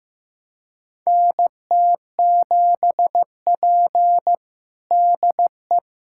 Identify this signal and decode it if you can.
Morse code, heard as a keyed tone, about 15 wpm, 700 Hz. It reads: NT7P DE
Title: USA Callsign DE Callsign - 4 Characters - 15wpm